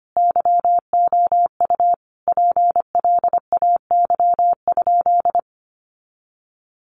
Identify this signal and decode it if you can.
Morse code, heard as a keyed tone, about 25 wpm, 700 Hz. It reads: YOU PLAY?
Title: Sentences from Top 1400 Words 25wpm